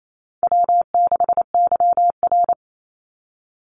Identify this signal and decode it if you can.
Morse code, heard as a keyed tone, about 28 wpm, 700 Hz. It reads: W6YR
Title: Call Signs Repeated 3x - 28wpm